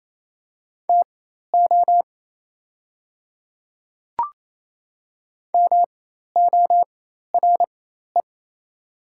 Morse code, heard as a keyed tone, about 27 wpm, spaced as for 13 wpm, 700 Hz. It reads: TO MORE
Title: 4x Character Spacing - Top 100 Words - 28wpm